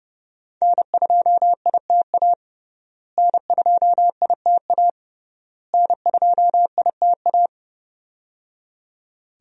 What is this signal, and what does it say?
Morse code, heard as a keyed tone, about 31 wpm, 700 Hz. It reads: N2ITA N2ITA N2ITA
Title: USA Call Signs - 5 Characters - Repeated 3x - 30wpm